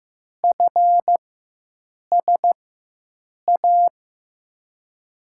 Morse code, heard as a keyed tone, about 16 wpm, spaced as for 7 wpm, 700 Hz. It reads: FSA
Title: USA Call Signs - 6 Characters - 4x Character Spacing - 15wpm